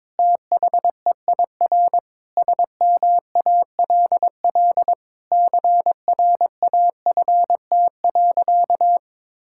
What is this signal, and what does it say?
Morse code, heard as a keyed tone, about 22 wpm, 700 Hz. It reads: THEIR SMALL CRAFT.
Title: Sentences from Top 1000 Words - Review Part 6 - 22wpm